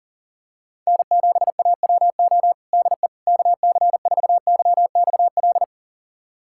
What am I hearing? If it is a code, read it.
Morse code, heard as a keyed tone, about 37 wpm, 700 Hz. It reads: N7AWO DE KC4YXL
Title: USA Callsign DE Callsign 40wpm